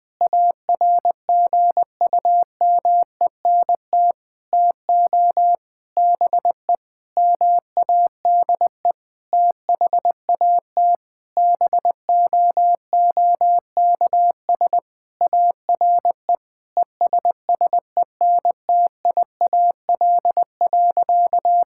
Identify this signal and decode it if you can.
Morse code, heard as a keyed tone, about 20 wpm, 700 Hz. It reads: ARGUMENT TO BE MADE THAT BOOKS ARE ESSENTIAL.
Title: Sentences with 8-Word Phrases - 20wpm